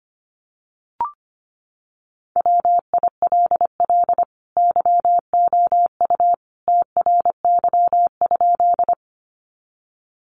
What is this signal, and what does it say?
Morse code, heard as a keyed tone, about 25 wpm, 700 Hz. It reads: WILL YOU TRY?
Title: Sentences from Top 200 Words 25wpm